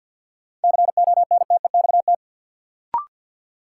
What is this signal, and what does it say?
Morse code, heard as a keyed tone, about 53 wpm, 700 Hz. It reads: CONTEXT
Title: Sentences from Top 1000 Words - Review Part 7 - New Word Drill - 50wpm